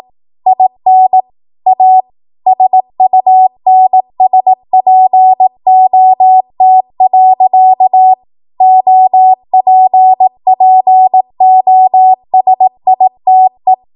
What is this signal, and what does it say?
Morse code, heard as a keyed tone, about 18 wpm, 750 Hz. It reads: IN A SUNSPOT. OPPOSITE
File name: Morse / 18WPM